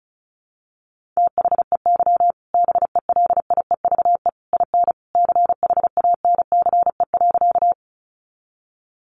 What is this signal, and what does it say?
Morse code, heard as a keyed tone, about 34 wpm, 700 Hz. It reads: THEY BELIEVE IN CHANCE.
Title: Sentences from Top 1000 Words 35wpm